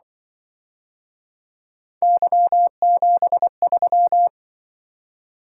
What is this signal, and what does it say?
Morse code, heard as a keyed tone, about 25 wpm, 700 Hz. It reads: Y73